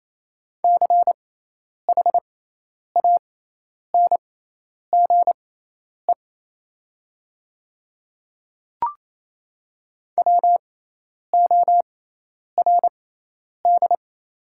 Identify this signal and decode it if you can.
Morse code, heard as a keyed tone, about 28 wpm, spaced as for 10 wpm, 700 Hz. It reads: CHANGE WORD
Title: 6x Character Spacing - Top 300 Words - 28wpm